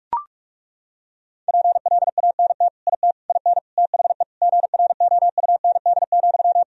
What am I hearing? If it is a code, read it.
Morse code, heard as a keyed tone, about 41 wpm, 700 Hz. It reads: PLANT IT IN THE GROUND,
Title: Sentences from Top 1000 Words - Review Part 2 - 45wpm